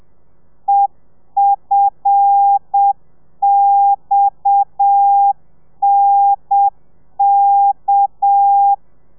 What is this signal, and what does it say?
Morse code, heard as a keyed tone, about 7 wpm, 800 Hz. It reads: EFXNK
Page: Lezione 1: audio a 30 CPM